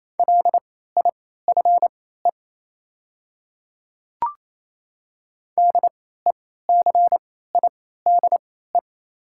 Morse code, heard as a keyed tone, about 28 wpm, spaced as for 16 wpm, 690 Hz. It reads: LIFE DECIDE
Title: 3x Character Spacing - Top 500 Words - 28wpm